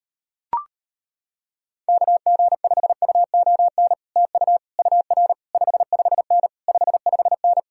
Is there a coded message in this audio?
Yes: Morse code from 1.9 to 7.6 s, heard as a keyed tone, about 33 wpm, 700 Hz. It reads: KG5UON TU UR 55N 55N